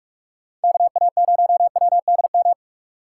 Morse code, heard as a keyed tone, about 45 wpm, 700 Hz. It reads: KA0WDM